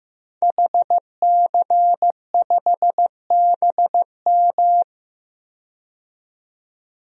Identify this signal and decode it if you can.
Morse code, heard as a keyed tone, about 15 wpm, 700 Hz. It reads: HC5BM
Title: DXCC-100 Call Signs - 5 Characters - 15wpm